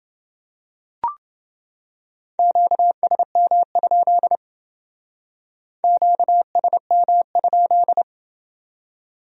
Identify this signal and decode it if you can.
Morse code, heard as a keyed tone, about 28 wpm, 700 Hz. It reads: QSM? QSM?